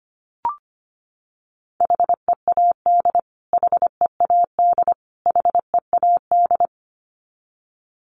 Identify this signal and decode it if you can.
Morse code, heard as a keyed tone, about 25 wpm, 700 Hz. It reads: HEAD HEAD HEAD
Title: Top 500 Words Repeated 25wpm